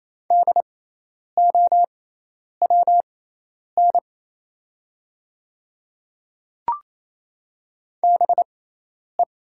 Morse code, heard as a keyed tone, about 28 wpm, spaced as for 10 wpm, 700 Hz. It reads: DOWN BE
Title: 6x Character Spacing - Top 100 Words - 28wpm